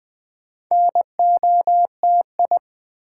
Morse code, heard as a keyed tone, about 20 wpm, 700 Hz. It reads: NOTI
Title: Build A Word - 2 Letters At A Time - 6-Letter Words - 20wpm